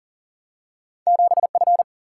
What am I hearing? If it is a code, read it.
Morse code, heard as a keyed tone, about 40 wpm, 700 Hz. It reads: 7F
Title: Single-Letter Single-Number 40wpm